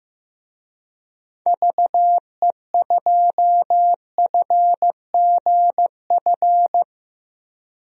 Morse code, heard as a keyed tone, about 15 wpm, 700 Hz. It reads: VE2FGF
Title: DXCC-100 Call Signs - 6 Characters - Repeated 2x - 15wpm